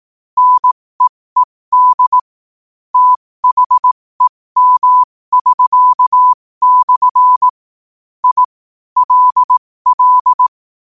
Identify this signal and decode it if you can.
Morse code, heard as a keyed tone, about 18 wpm, spaced as for 15 wpm, 1000 Hz. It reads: NEED THEM<SK>/ I LL
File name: qotdfiles / 15wpm